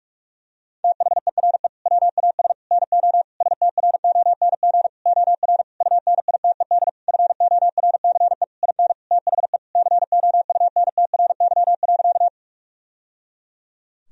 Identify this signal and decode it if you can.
Morse code, heard as a keyed tone, about 43 wpm, 700 Hz. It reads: THERE WAS NO STRONG OR UNITED FORCE IN THE COUNTRY.
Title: Sentences from Top 1000 Words - Review Part 4 - 45wpm